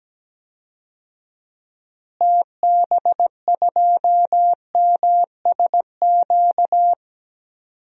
Morse code, heard as a keyed tone, about 17 wpm, 700 Hz. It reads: TB2MSQ
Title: DXCC-100 Call Signs - 6 Characters - Repeated 3x - 17wpm